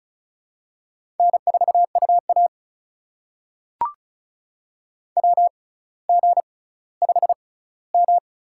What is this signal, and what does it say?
Morse code, heard as a keyed tone, about 36 wpm, spaced as for 18 wpm, 700 Hz. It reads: N4UA WG5M
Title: USA Call Signs - 4 Characters - 6x Character Spacing - 35wpm